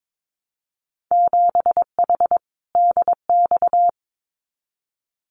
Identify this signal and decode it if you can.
Morse code, heard as a keyed tone, about 22 wpm, 700 Hz. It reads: 7H DX
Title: Winter Field Day Exchanges Encore 22wpm